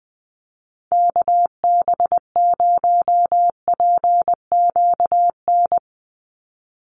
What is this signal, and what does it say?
Morse code, heard as a keyed tone, about 20 wpm, 700 Hz. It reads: KB0PQN
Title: USA Call Signs 20wpm